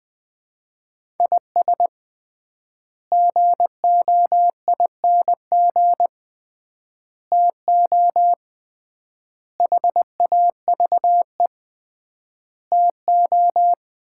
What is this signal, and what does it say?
Morse code, heard as a keyed tone, about 20 wpm, 700 Hz. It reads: IS GOING TO HAVE TO
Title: 3x Word-Spacing - Sets of 5 Words - 20wpm